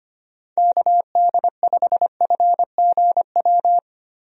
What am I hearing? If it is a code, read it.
Morse code, heard as a keyed tone, about 25 wpm, 700 Hz. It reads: KD5FGW
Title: USA Call Signs - 6 Characters - 25wpm